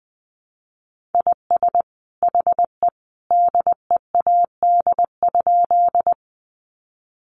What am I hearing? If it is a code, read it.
Morse code, heard as a keyed tone, about 20 wpm, 700 Hz. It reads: IS HE DEAD?
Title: Sentences from Top 1500 Words 20wpm